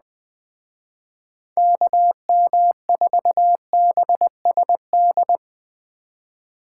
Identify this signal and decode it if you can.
Morse code, heard as a keyed tone, about 20 wpm, 700 Hz. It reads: KM4BSD
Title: Speed Racing - USA 2x3 Call Signs 17wpm